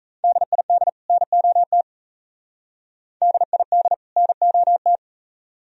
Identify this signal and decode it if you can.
Morse code, heard as a keyed tone, about 35 wpm, 700 Hz. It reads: DID NOT DID NOT
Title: Speed-Racing - Sets of 2 Words 28wpm